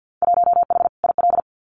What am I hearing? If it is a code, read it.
Morse code, heard as a keyed tone, about 48 wpm, 700 Hz. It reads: 1H IL
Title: Winter Field Day Exchanges 50wpm